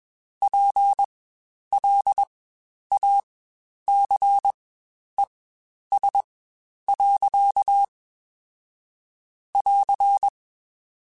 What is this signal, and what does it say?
Morse code, heard as a keyed tone, about 21 wpm, spaced as for 10 wpm, 800 Hz. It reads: PLACES. <AR>
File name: Spaced / Chuckles Spaced-mp3s / MiddleAge sp @20